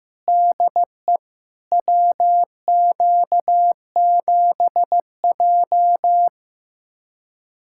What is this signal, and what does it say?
Morse code, heard as a keyed tone, about 15 wpm, 700 Hz. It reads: DE WQ7J
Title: USA Callsign DE Callsign - 4 Characters - 15wpm